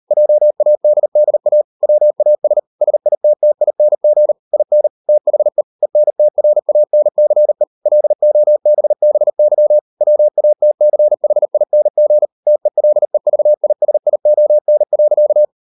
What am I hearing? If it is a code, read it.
Morse code, heard as a keyed tone, about 38 wpm, 600 Hz. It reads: JADDA WAS SITTING IN THE ENTRANCE LOBBY WATCHING TELEVISION.